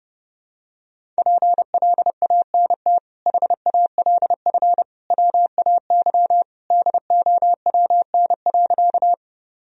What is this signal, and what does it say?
Morse code, heard as a keyed tone, about 30 wpm, 700 Hz. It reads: PLANT HALF WAY DOWN.
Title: Sentences from Top 1000 Words - Review Part 1 - 30wpm